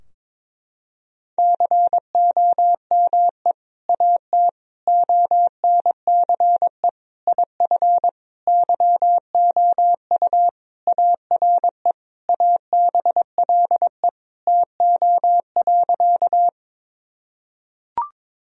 Morse code, heard as a keyed tone, about 22 wpm, 700 Hz. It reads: COME AT ONCE IF YOU ARE ABLE TO.
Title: Sentences from Top 1000 Words - Review Part 2 - 22wpm